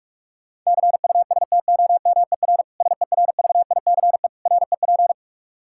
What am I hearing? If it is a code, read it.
Morse code, heard as a keyed tone, about 46 wpm, 700 Hz. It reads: CUSTOMER SERVICE REP
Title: Practice QSO - Exchange 3 - More Info - 45wpm